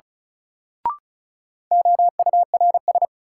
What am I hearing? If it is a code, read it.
Morse code, heard as a keyed tone, about 36 wpm, 700 Hz. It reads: OURS
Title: Sentences from Top 1000 Words - Review Part 5 - New Word Drill - 35wpm